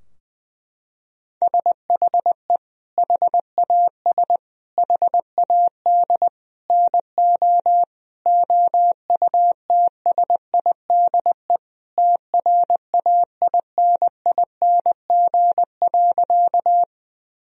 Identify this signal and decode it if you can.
Morse code, heard as a keyed tone, about 20 wpm, 700 Hz. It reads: SHE HAS HAD NO OUTSIDE TRAINING.
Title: Sentences from Top 1000 Words - Review Part 4 - 20wpm